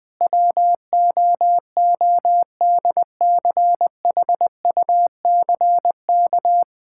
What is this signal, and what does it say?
Morse code, heard as a keyed tone, about 21 wpm, 700 Hz. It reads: WOODCHUCK
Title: Pangram Challenge v2 - 20wpm